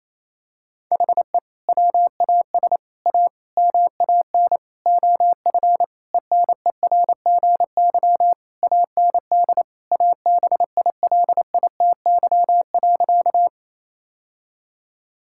Morse code, heard as a keyed tone, about 28 wpm, 700 Hz. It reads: HE WAS A MAN OF ENERGY AND ABILITY.
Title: Sentences from Top 1000 Words - Review Part 6 - 28wpm